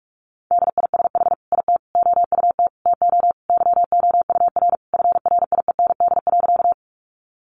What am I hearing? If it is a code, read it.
Morse code, heard as a keyed tone, about 44 wpm, 700 Hz. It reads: DISH IT OUT TO YOUR FRIEND.
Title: Sentences from Top 1300 Words 45wpm